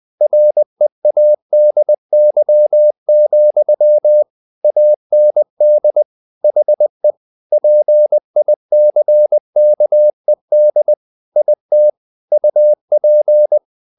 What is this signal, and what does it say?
Morse code, heard as a keyed tone, about 20 wpm, 600 Hz. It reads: READY, AND HE PICKED IT UP
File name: the old man and the sea part 2 20wpm